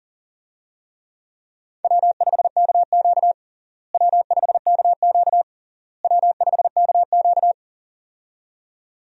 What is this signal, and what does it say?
Morse code, heard as a keyed tone, about 40 wpm, 700 Hz. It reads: W5KQ W5KQ W5KQ
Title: USA Call Signs - 4 Characters - Repeated 3x - 40wpm